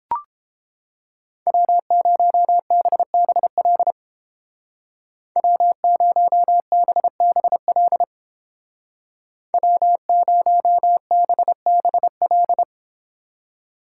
Morse code, heard as a keyed tone, about 29 wpm, 700 Hz. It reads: W0BBL W0BBL W0BBL